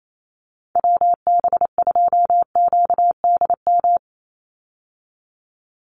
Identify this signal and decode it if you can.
Morse code, heard as a keyed tone, about 28 wpm, 700 Hz. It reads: WB2QDM
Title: USA Call Signs 28wpm